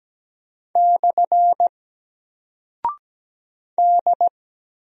Morse code, heard as a keyed tone, about 17 wpm, 700 Hz. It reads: / D
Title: Single Character (Letter:Number:Punctuation:Prosign) 17wpm